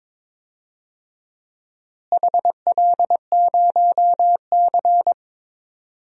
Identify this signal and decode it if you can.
Morse code, heard as a keyed tone, about 22 wpm, 700 Hz. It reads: HL0C
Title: DXCC-100 Call Signs - 4 Characters - Repeated 3x - 22wpm